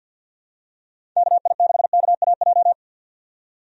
Morse code, heard as a keyed tone, about 50 wpm, 700 Hz. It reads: KI6KAJ
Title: USA Call Signs - 6 Characters - 50wpm